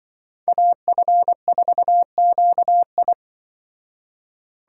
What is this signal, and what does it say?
Morse code, heard as a keyed tone, about 24 wpm, 700 Hz. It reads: AF4QI
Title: Speed Racing - USA Call Signs 20wpm